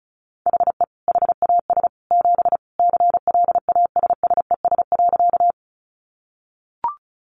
Morse code, heard as a keyed tone, about 34 wpm, 700 Hz. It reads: HE HAS 7 CLASSES.
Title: Additional Easy Phrases 35wpm